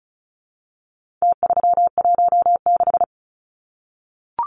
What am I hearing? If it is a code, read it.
Morse code, heard as a keyed tone, about 34 wpm, 700 Hz. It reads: T316
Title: Numbers - Contest Style - 4-Digit 35wpm